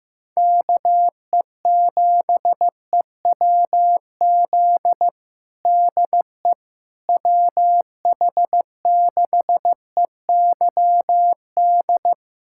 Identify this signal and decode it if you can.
Morse code, heard as a keyed tone, about 16 wpm, 700 Hz. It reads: KE7EWZ DE WH6EYD
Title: USA Callsign DE Callsign 15wpm